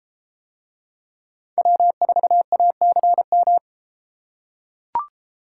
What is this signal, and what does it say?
Morse code, heard as a keyed tone, about 30 wpm, 700 Hz. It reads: W4ACM